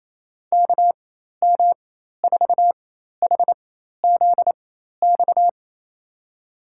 Morse code, heard as a keyed tone, about 28 wpm, spaced as for 13 wpm, 700 Hz. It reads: KM4HZX
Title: USA Call Signs - 6 Characters - 4x Character Spacing - 28wpm